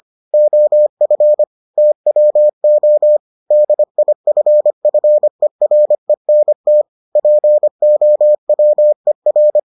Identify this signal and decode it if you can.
Morse code, heard as a keyed tone, about 25 wpm, 600 Hz. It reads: OF TWO DIFFERENT POWER